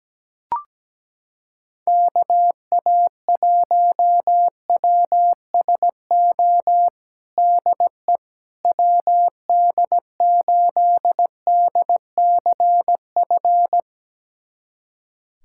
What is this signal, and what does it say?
Morse code, heard as a keyed tone, about 17 wpm, 700 Hz. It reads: KA1WSO DE WD8DCF
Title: USA Callsign DE Callsign - 6 Characters - 17wpm